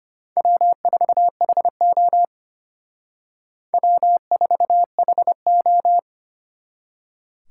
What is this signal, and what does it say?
Morse code, heard as a keyed tone, about 32 wpm, 700 Hz. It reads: W4HO W4HO